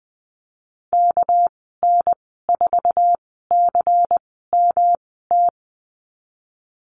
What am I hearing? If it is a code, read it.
Morse code, heard as a keyed tone, about 19 wpm, spaced as for 14 wpm, 700 Hz. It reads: KN4CMT
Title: 213 Lesson 34g - Number 8 - Call Signs 20wpm